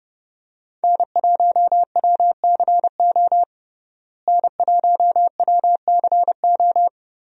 Morse code, heard as a keyed tone, about 30 wpm, 700 Hz. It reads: N1WCO N1WCO
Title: USA Call Signs - 5 Characters - Repeated 3x - 30wpm